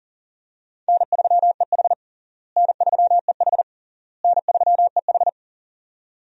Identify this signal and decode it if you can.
Morse code, heard as a keyed tone, about 40 wpm, 700 Hz. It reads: N3EH N3EH N3EH
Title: USA Call Signs - 4 Characters - Repeated 3x - 40wpm